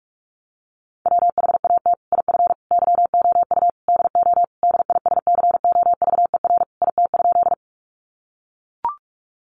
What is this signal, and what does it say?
Morse code, heard as a keyed tone, about 44 wpm, 700 Hz. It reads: WHAT IF YOU DO DISCOVER IT?
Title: Sentences from Top 1100 Words 45wpm